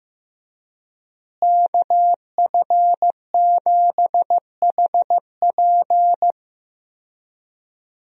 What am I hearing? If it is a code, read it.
Morse code, heard as a keyed tone, about 16 wpm, 700 Hz. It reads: KF7HP
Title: USA Call Signs - 5 Characters - Repeated 3x - 15wpm